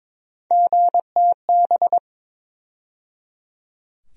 Morse code, heard as a keyed tone, about 22 wpm, 700 Hz. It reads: GTB
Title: USA Call Sign Postfixes - 3 Letters 22wpm